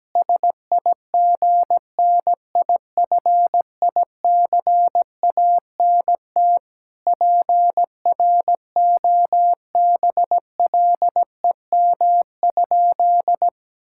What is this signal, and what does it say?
Morse code, heard as a keyed tone, about 17 wpm, 700 Hz. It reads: SIGNIFICANT PROBLEM?
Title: Sentences from Top 1000 Words - Review Part 6 - 17wpm